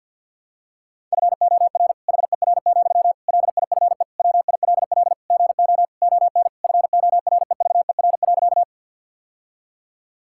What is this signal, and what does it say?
Morse code, heard as a keyed tone, about 50 wpm, 700 Hz. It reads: FOR HER, LIFE WILL GO ON FOREVER.
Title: Sentences from Top 1600 Words 50wpm